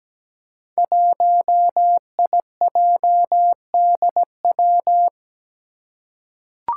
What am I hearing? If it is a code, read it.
Morse code, heard as a keyed tone, about 17 wpm, 700 Hz. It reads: 1IJDW
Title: Five Challenging Letter-Number Combinations 17wpm